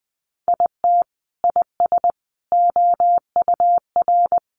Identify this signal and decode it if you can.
Morse code, heard as a keyed tone, about 20 wpm, 700 Hz. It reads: IT IS OUR
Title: Sentences from Top 1500 Words 20wpm